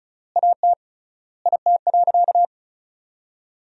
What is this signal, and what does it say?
Morse code, heard as a keyed tone, about 36 wpm, 700 Hz. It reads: AT IT.
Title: 3x Word-Spacing - Sentences from Top 300 Words - 35wpm